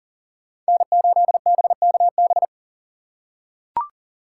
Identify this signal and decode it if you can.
Morse code, heard as a keyed tone, about 42 wpm, 700 Hz. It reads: N8BKB
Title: USA Call Signs - 5 Characters - 6x Character Spacing - 40wpm